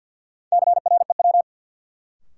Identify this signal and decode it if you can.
Morse code, heard as a keyed tone, about 53 wpm, 700 Hz. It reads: CREW
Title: Sentences from Top 1500 Words - New Word Drill - 50wpm